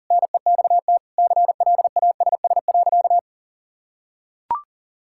Morse code, heard as a keyed tone, about 40 wpm, 700 Hz. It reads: NEXT CLASS.
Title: Sentences from Top 1000 Words - Review Part 6 - 40wpm